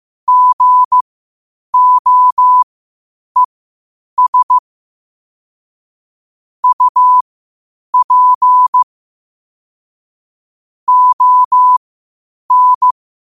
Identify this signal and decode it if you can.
Morse code, heard as a keyed tone, about 15 wpm, spaced as for 8 wpm, 1000 Hz. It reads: GOES UP ON
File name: qotdfiles / 8wpm